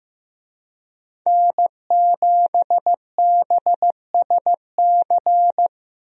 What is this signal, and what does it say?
Morse code, heard as a keyed tone, about 16 wpm, 700 Hz. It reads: N7BSC
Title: USA Call Signs - 5 Characters - Repeated 2x - 15wpm